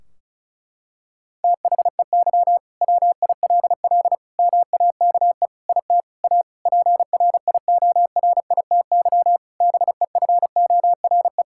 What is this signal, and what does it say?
Morse code, heard as a keyed tone, about 36 wpm, 700 Hz. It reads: THEY WILL MAKE IT A PRIORITY BEFORE
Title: Sentences from Top 1600 Words 35wpm